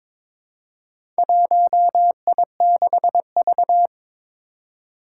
Morse code, heard as a keyed tone, about 22 wpm, 700 Hz. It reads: 1I6V
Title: Four Challenging Letter-Number Combinations 22wpm